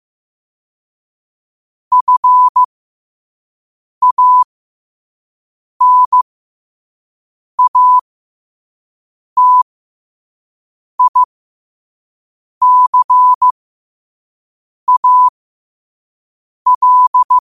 Morse code, heard as a keyed tone, about 15 wpm, spaced as for 5 wpm, 1000 Hz. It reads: FANATICAL
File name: qotdfiles / 5wpm